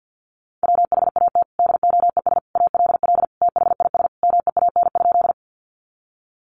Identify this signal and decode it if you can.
Morse code, heard as a keyed tone, about 50 wpm, 700 Hz. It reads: WHAT DOES ALL THIS MEAN?
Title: Sentences from Top 200 Words 50wpm